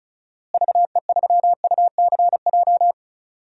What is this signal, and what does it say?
Morse code, heard as a keyed tone, about 34 wpm, 700 Hz. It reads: VE3UCJ
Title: DXCC-100 Call Signs - 6 Characters - Repeated 3x - 35wpm